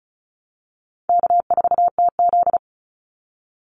Morse code, heard as a keyed tone, about 34 wpm, 700 Hz. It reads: K4TZ
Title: Call Signs Repeated 3x - 35wpm